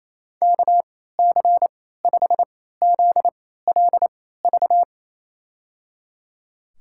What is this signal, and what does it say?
Morse code, heard as a keyed tone, about 27 wpm, spaced as for 16 wpm, 700 Hz. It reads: KC5ZLV
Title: USA Call Signs - 3x Character Spacing - 28wpm